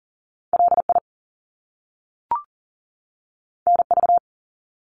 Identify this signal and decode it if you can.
Morse code, heard as a keyed tone, about 40 wpm, 700 Hz. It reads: LI NV
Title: Top Two Letter Combinations in Words 40wpm